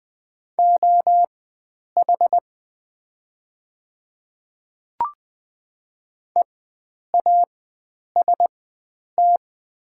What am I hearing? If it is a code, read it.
Morse code, heard as a keyed tone, about 20 wpm, spaced as for 9 wpm, 700 Hz. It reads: OH EAST
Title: 4x Character Spacing - Top 500 Words - 20wpm